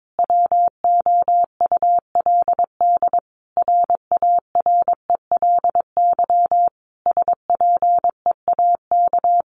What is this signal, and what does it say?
Morse code, heard as a keyed tone, about 22 wpm, 700 Hz. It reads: WOULD RARELY SPEAK
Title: Sentences from Top 1400 Words 22wpm